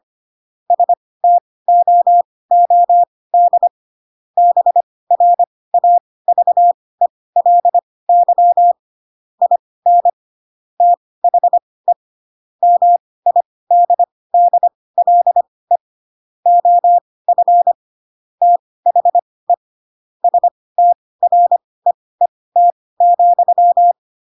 Morse code, heard as a keyed tone, about 25 wpm, spaced as for 18 wpm, 700 Hz. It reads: STOOD BRAVELY IN THE MIDDLE OF THE STREET,
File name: Pinocchio ch3 12 25 ik1hge